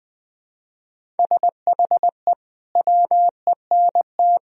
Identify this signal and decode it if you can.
Morse code, heard as a keyed tone, about 20 wpm, 700 Hz. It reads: SHE WENT
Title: Sentences from Top 1000 Words - Review Part 1 - 20wpm